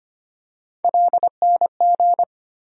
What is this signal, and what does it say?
Morse code, heard as a keyed tone, about 25 wpm, 700 Hz. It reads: LNG
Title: USA Call Sign Postfixes - 3 Letters 25wpm